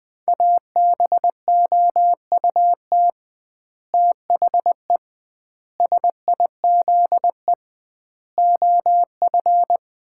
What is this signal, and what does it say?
Morse code, heard as a keyed tone, about 20 wpm, 700 Hz. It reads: ABOUT THE SIZE OF
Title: 2x Word-Spacing - Sets of 5 Words - 20wpm